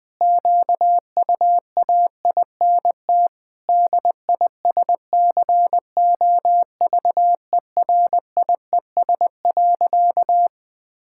Morse code, heard as a keyed tone, about 20 wpm, 700 Hz. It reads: QUAINT DISCOVERIES.
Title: Pangram Challenge v2 - 20wpm